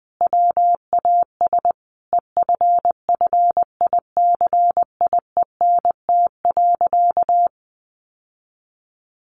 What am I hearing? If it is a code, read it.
Morse code, heard as a keyed tone, about 20 wpm, 700 Hz. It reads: WAS EFFICIENT.
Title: Sentences from Top 1300 Words 20wpm